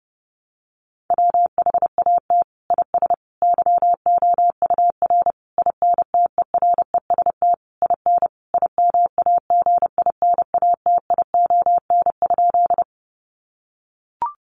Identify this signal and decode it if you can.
Morse code, heard as a keyed tone, about 30 wpm, 700 Hz. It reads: WHAT IS YOUR INTEREST IN IMAGINATION?
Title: Sentences from Top 1300 Words 30wpm